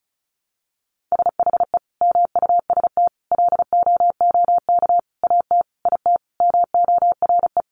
Morse code, heard as a keyed tone, about 34 wpm, 700 Hz. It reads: SHE MUST LOOK AT IT MORE
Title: Sentences from Top 1400 Words 35wpm